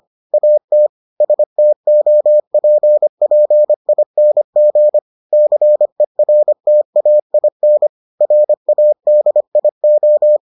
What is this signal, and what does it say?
Morse code, heard as a keyed tone, about 25 wpm, 600 Hz. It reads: AT STOPPING CERTAIN RADIO